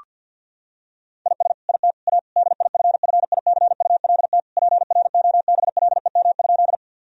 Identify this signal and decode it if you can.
Morse code, heard as a keyed tone, about 52 wpm, 700 Hz. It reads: IS IT A DIFFICULT PROBLEM?
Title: Sentences from Top 1000 Words - Review Part 2 - 50wpm